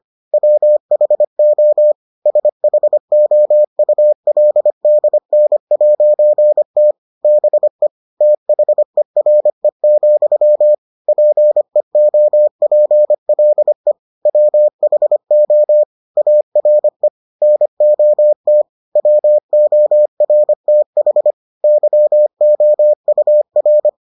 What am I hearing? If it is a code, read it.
Morse code, heard as a keyed tone, about 25 wpm, 600 Hz. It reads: WHO SHOULDN'T BE THERE, PEOPLE WHO ARE NOT WORTH YOUR